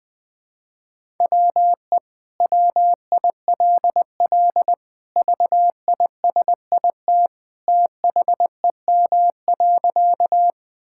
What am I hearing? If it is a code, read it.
Morse code, heard as a keyed tone, about 21 wpm, 700 Hz. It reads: WE WILL VISIT THEM.
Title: Sentences from Top 1000 Words - Review Part 6 - 20wpm